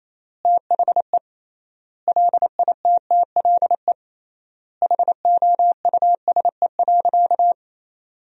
Morse code, heard as a keyed tone, about 28 wpm, 700 Hz. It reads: THE LITTLE HOUSE.
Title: 3x Word-Spacing - Sentences from Top 200 Words - 28wpm